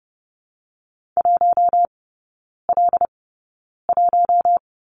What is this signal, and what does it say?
Morse code, heard as a keyed tone, about 30 wpm, spaced as for 9 wpm, 700 Hz. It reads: 1L1
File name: 081 Lesson 14e - Number 2 - Three Characters 30wpm